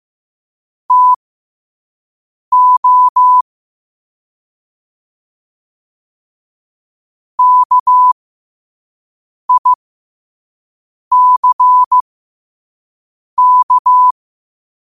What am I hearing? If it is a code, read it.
Morse code, heard as a keyed tone, about 15 wpm, spaced as for 5 wpm, 1000 Hz. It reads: TO KICK
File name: qotdfiles / 5wpm